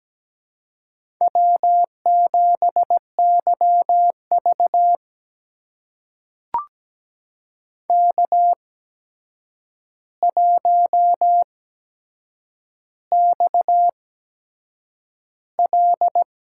Morse code, heard as a keyed tone, about 17 wpm, spaced as for 7 wpm, 700 Hz. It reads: W7YV K1XL
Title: USA Call Signs - 4 Characters - 8x Character Spacing - 17wpm